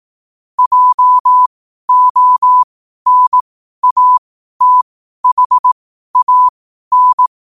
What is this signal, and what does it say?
Morse code, heard as a keyed tone, about 18 wpm, spaced as for 12 wpm, 1000 Hz. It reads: JONATHAN
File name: qotdfiles / 12wpm